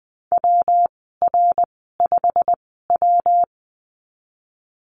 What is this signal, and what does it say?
Morse code, heard as a keyed tone, about 20 wpm, spaced as for 15 wpm, 700 Hz. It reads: WR5W
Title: 167 Lesson 27g - Number 7 - Call Signs 20wpm